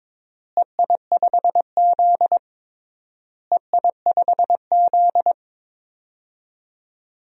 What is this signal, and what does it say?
Morse code, heard as a keyed tone, about 22 wpm, 700 Hz. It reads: EI5Z EI5Z
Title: DXCC-100 Call Signs - 4 Characters - Repeated 3x - 22wpm